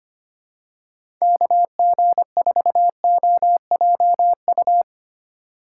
Morse code, heard as a keyed tone, about 25 wpm, 700 Hz. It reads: KG4OJU
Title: USA Call Signs - 6 Characters - 8x Character Spacing - 25wpm